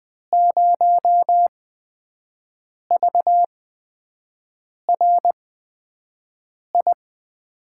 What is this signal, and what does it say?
Morse code, heard as a keyed tone, about 20 wpm, spaced as for 5 wpm, 700 Hz. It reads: 0VRI
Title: USA Call Signs - 6 Characters - 8x Character Spacing - 20wpm